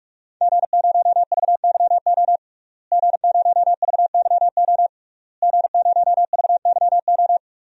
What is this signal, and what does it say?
Morse code, heard as a keyed tone, about 42 wpm, 700 Hz. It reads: G0VYO G0VYO G0VYO
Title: DXCC-100 Call Signs - 5 Characters - Repeated 3x - 45wpm